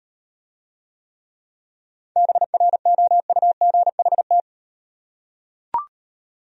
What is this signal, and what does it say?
Morse code, heard as a keyed tone, about 38 wpm, 700 Hz. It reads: BROUGHT